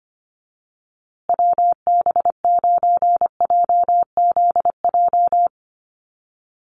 Morse code, heard as a keyed tone, about 25 wpm, 700 Hz. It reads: WB9JZJ
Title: USA Call Signs Repeated 3x 25wpm